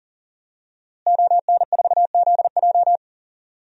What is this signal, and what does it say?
Morse code, heard as a keyed tone, about 40 wpm, 700 Hz. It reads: ON4ZJ
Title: DXCC-100 Call Signs - 5 Characters - Repeated 2x - 40wpm